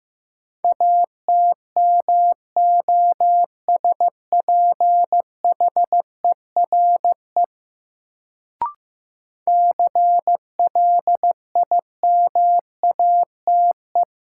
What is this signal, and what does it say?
Morse code, heard as a keyed tone, about 15 wpm, 700 Hz. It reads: ATMOSPHERE CLIMATE
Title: Sentences from Top 1600 Words - New Word Drill - 15wpm